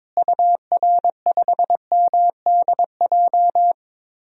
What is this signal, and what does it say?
Morse code, heard as a keyed tone, about 21 wpm, 700 Hz. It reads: UR5MDJ
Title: DXCC-100 Call Signs - 6 Characters - Repeated 3x - 22wpm